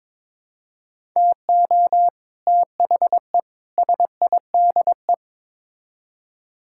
Morse code, heard as a keyed tone, about 22 wpm, 700 Hz. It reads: TO THE SIDE
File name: practice / 22wpm / Build A Phrase - 5-Word Phrases - 22wpm